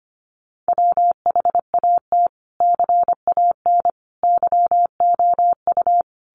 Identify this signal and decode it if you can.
Morse code, heard as a keyed tone, about 25 wpm, 700 Hz. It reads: WHAT CAN YOU
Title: Sets of 5 Words Encore 25wpm